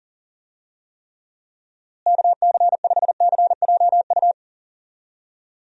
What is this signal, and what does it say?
Morse code, heard as a keyed tone, about 40 wpm, 700 Hz. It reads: KC5CJU
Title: USA Call Signs - 6 Characters - Repeated 3x - 40wpm